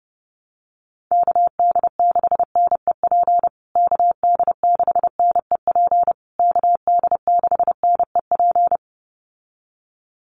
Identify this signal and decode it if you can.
Morse code, heard as a keyed tone, about 30 wpm, 700 Hz. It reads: KD6NEP KD6NEP KD6NEP
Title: USA Call Signs Repeated 3x 30wpm